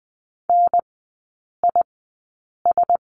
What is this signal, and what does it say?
Morse code, heard as a keyed tone, about 20 wpm, spaced as for 8 wpm, 700 Hz. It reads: NIS
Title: 035 Lesson 7e - Letter s - Three Characters 20wpm